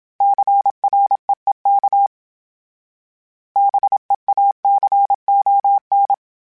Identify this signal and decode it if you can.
Morse code, heard as a keyed tone, about 27 wpm, 800 Hz. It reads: CREEK BEACON
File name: fromSMs / FromSMs mp3s / Cities@25